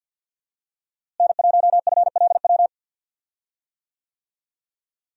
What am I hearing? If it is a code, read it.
Morse code, heard as a keyed tone, about 52 wpm, 700 Hz. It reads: N1FLW
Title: USA Call Signs - 5 Characters - Repeated 3x - 50wpm